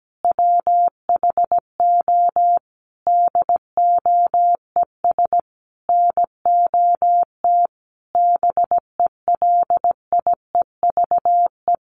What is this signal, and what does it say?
Morse code, heard as a keyed tone, about 17 wpm, 700 Hz. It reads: WHO DOES NOT BELIEVE
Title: Sentences from Top 1000 Words 17wpm